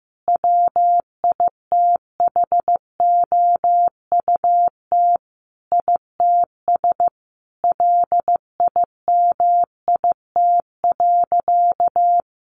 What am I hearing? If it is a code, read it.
Morse code, heard as a keyed tone, about 15 wpm, 700 Hz. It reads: WITHOUT ITS LIMIT.
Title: Sentences from Top 1500 Words 15wpm